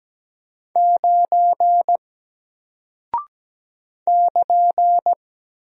Morse code, heard as a keyed tone, about 17 wpm, 700 Hz. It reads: 9 <KN>
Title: Single Character (Letter:Number:Punctuation:Prosign) Alt 17wpm